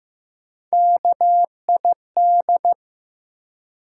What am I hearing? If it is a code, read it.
Morse code, heard as a keyed tone, about 15 wpm, 700 Hz. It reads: KID
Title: Three Letter Words - 15wpm